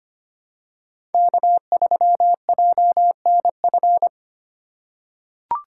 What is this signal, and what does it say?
Morse code, heard as a keyed tone, about 25 wpm, 700 Hz. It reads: K3JNF
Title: USA Call Signs - 5 Characters - 3x Character Spacing - 25wpm